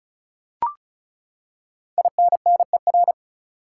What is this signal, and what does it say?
Morse code, heard as a keyed tone, about 35 wpm, 700 Hz. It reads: INNER
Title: Five Letter Words - 35wpm